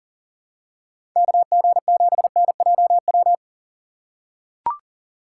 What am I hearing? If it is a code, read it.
Morse code, heard as a keyed tone, about 37 wpm, 700 Hz. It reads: KG7NJW
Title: USA Call Signs - 6 Characters - 6x Character Spacing - 40wpm